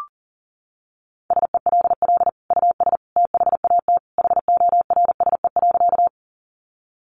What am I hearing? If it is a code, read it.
Morse code, heard as a keyed tone, about 40 wpm, 700 Hz. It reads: SELL US THAT HORSE.
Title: Sentences from Top 1100 Words 40wpm